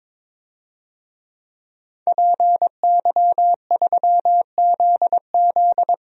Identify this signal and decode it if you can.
Morse code, heard as a keyed tone, about 22 wpm, 700 Hz. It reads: PY3ZZ
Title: DXCC-100 Call Signs - 5 Characters - Repeated 2x - 22wpm